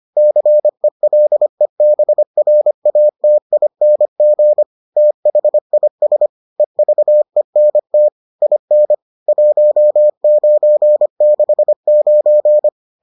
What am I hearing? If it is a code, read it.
Morse code, heard as a keyed tone, about 25 wpm, 600 Hz. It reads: CELEBRATING THIS EVENT IN 1969